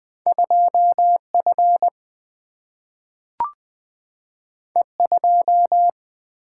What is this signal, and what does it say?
Morse code, heard as a keyed tone, about 20 wpm, 700 Hz. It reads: 2F E2
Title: Single-Letter Single-Number 20wpm